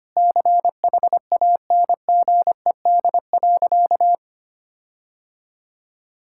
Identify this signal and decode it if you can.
Morse code, heard as a keyed tone, about 25 wpm, 700 Hz. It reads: CHANGED.
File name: practice / 25wpm / Sentences with 4-Word Phrases - 25wpm